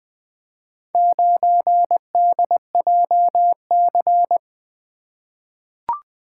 Morse code, heard as a keyed tone, about 20 wpm, 700 Hz. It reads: 9DJC
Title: Four Letter-Number Combinations - 20wpm